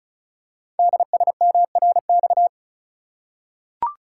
Morse code, heard as a keyed tone, about 36 wpm, 700 Hz. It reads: DSMRX
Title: Code Groups - 5 Letters - 35wpm